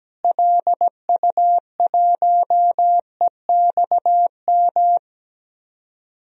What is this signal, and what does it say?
Morse code, heard as a keyed tone, about 17 wpm, 700 Hz. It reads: LU1EXM
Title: DXCC-100 Call Signs - 6 Characters - Repeated 3x - 17wpm